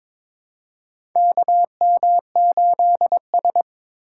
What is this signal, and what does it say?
Morse code, heard as a keyed tone, about 22 wpm, 700 Hz. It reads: KM8S
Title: USA Call Signs - 4 Characters - 6x Character Spacing - 22wpm